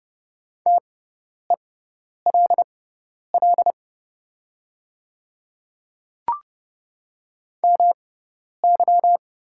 Morse code, heard as a keyed tone, about 30 wpm, spaced as for 10 wpm, 690 Hz. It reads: TELL MY
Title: 6x Character Spacing - Top 200 Words - 30wpm